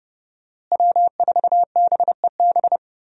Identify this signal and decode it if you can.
Morse code, heard as a keyed tone, about 31 wpm, 700 Hz. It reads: W4BEB
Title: USA Call Signs - 5 Characters - Repeated 2x - 30wpm